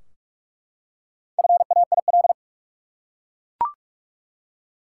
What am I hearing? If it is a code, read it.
Morse code, heard as a keyed tone, about 41 wpm, 700 Hz. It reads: FAIL
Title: Four Letter Words - 45wpm